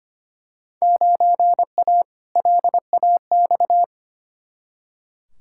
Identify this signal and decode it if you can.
Morse code, heard as a keyed tone, about 25 wpm, 700 Hz. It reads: 9A LAX
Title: Field Day Exchanges Encore - 25wpm